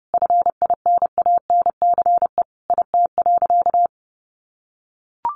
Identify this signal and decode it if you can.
Morse code, heard as a keyed tone, about 30 wpm, 700 Hz. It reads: FINANCE IT.
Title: Sentences from Top 1500 Words 30wpm